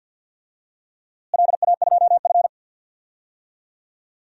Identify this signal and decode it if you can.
Morse code, heard as a keyed tone, about 52 wpm, 700 Hz. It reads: LA2F
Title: DXCC-100 Call Signs - 4 Characters - Repeated 2x - 50wpm